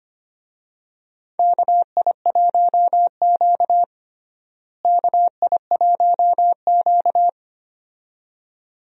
Morse code, heard as a keyed tone, about 25 wpm, 700 Hz. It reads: KI1Q KI1Q
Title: USA Call Signs - 4 Characters - Repeated 2x - 25wpm